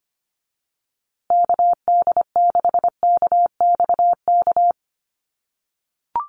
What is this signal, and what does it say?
Morse code, heard as a keyed tone, about 25 wpm, 700 Hz. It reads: KD6KXK
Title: USA Call Signs Repeated 2x 25wpm